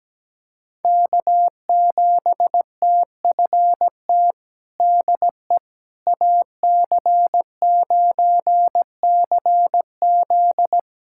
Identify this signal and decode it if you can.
Morse code, heard as a keyed tone, about 17 wpm, 700 Hz. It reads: K7TFT DE AC9CZ
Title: USA Callsign DE Callsign - 5 Characters - 17wpm